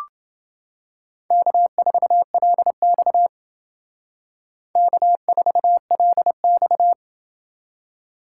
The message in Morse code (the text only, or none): K4LX K4LX